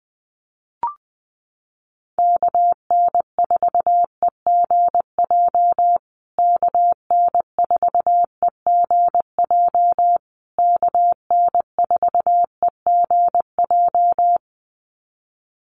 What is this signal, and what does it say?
Morse code, heard as a keyed tone, about 20 wpm, 700 Hz. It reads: KN4EGJ KN4EGJ KN4EGJ
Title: USA Call Signs Repeated 3x 20wpm